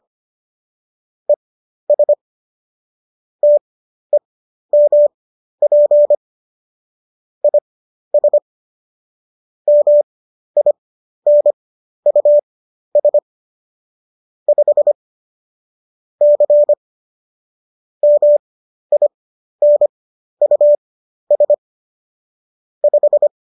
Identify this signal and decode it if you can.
Morse code, heard as a keyed tone, about 25 wpm, spaced as for 12 wpm, 590 Hz. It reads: ES TEMP IS MINUS 5 C MINUS 5
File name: qso012 12